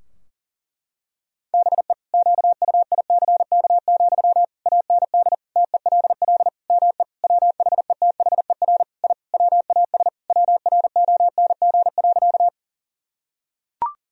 Morse code, heard as a keyed tone, about 40 wpm, 700 Hz. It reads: BE QUICK, AND TELL ME WHETHER I WAS WRONG.
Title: Sentences from Top 1000 Words - Review Part 3 - 40wpm